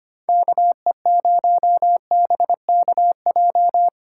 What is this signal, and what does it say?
Morse code, heard as a keyed tone, about 30 wpm, 700 Hz. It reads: KE0BKJ
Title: Speed Racing - USA 2x3 Call Signs 25wpm